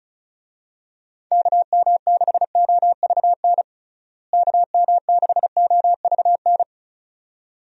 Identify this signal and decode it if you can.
Morse code, heard as a keyed tone, about 36 wpm, 700 Hz. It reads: KM6OVN KM6OVN
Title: USA Call Signs - 6 Characters - Repeated 2x - 35wpm